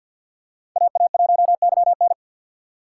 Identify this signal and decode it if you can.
Morse code, heard as a keyed tone, about 51 wpm, 700 Hz. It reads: AA1YN
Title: USA Call Signs - 5 Characters - 2x Character Spacing - 50wpm